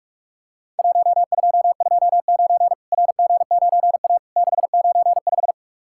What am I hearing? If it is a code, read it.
Morse code, heard as a keyed tone, about 46 wpm, 700 Hz. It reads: 1229 RG9A 695
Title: Worked All Europe DX Contest - QTC Exchange - 45wpm